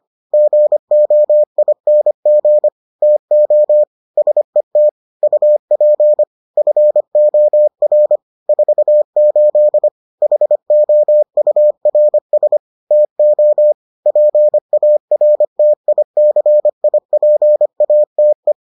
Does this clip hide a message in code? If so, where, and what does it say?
Yes: Morse code from 0.3 to 18.5 s, heard as a keyed tone, about 25 wpm, 600 Hz. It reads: GOING TO SET UP FOR 48 HOURS TO PARTICIPATE